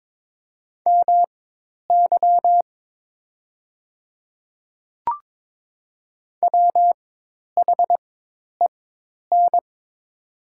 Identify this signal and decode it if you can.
Morse code, heard as a keyed tone, about 22 wpm, spaced as for 10 wpm, 700 Hz. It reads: MY WHEN
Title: 4x Character Spacing - Top 200 Words - 22wpm